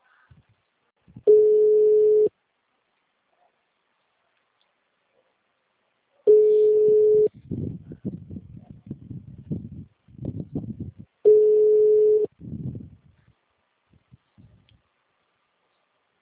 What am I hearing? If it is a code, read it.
Ringback tone (Europe, 425 Hz).